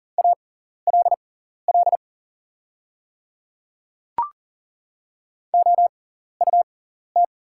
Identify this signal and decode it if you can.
Morse code, heard as a keyed tone, about 40 wpm, spaced as for 14 wpm, 700 Hz. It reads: ALL OUT
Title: 6x Character Spacing - Top 100 Words - 40wpm